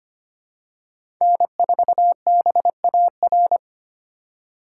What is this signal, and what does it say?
Morse code, heard as a keyed tone, about 25 wpm, 700 Hz. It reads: N4BAR
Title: USA Call Signs - 5 Characters - 2x Character Spacing - 25wpm